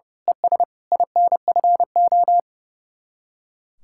Morse code, heard as a keyed tone, about 28 wpm, 700 Hz. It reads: ES INFO
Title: Speed-Racing Common QSO Phrases - 30wpm